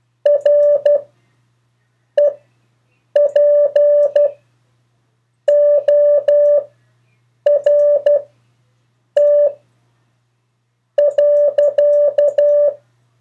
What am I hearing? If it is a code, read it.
Morse code, heard as a keyed tone, about 12 wpm, spaced as for 6 wpm, 580 Hz. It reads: REPORT.
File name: 4of4-sentences-mistakes-test